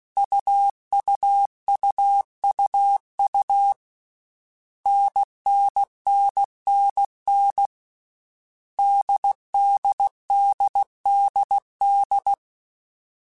Morse code, heard as a keyed tone, about 16 wpm, 800 Hz. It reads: UUUUU NNNNN DDDDD
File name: Basic Sets / Basic Letter Groups Collective7 / AWUNDG@13